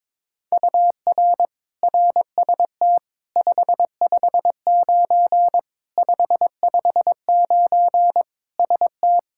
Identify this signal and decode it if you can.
Morse code, heard as a keyed tone, about 22 wpm, 700 Hz. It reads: UR RST 559 559 ST
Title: Practice QSO - Exchange 1 - Beginning - 22wpm